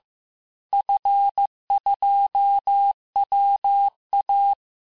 Morse code, heard as a keyed tone, about 15 wpm, 800 Hz. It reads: F2WA
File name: cours-du-lundi files / cours-1585-texte-FIN-de-QSO-vitesse-14mpm